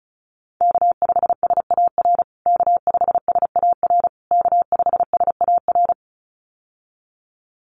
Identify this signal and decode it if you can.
Morse code, heard as a keyed tone, about 34 wpm, 700 Hz. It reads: K5SAR K5SAR K5SAR
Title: USA Call Signs Repeated 3x 35wpm